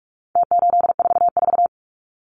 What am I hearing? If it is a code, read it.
Morse code, heard as a keyed tone, about 44 wpm, 700 Hz. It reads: T844
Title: Numbers - Contest Style - 4-Digit 45wpm